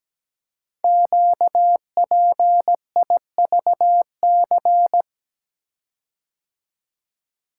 Morse code, heard as a keyed tone, about 17 wpm, 700 Hz. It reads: QPIVC
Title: Code Groups - 5 Letters - 17wpm